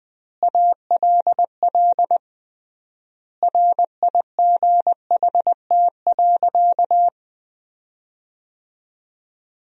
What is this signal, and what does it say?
Morse code, heard as a keyed tone, about 20 wpm, 700 Hz. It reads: ALL RIGHT.
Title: 3x Word-Spacing - Sentences from Top 500 Words - 20wpm